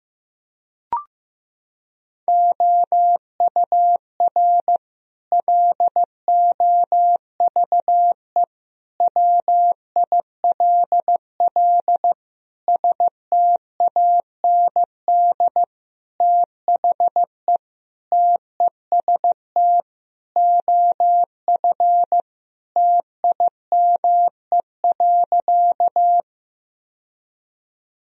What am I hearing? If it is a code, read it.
Morse code, heard as a keyed tone, about 15 wpm, 700 Hz. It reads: OUR LOVE WILL STAND THE TEST OF TIME.
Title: Sentences from Top 1000 Words - Review Part 3 - 15wpm